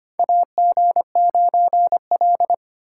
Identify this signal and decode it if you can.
Morse code, heard as a keyed tone, about 25 wpm, 700 Hz. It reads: AG9L
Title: USA Call Signs - 4 Characters - 3x Character Spacing - 25wpm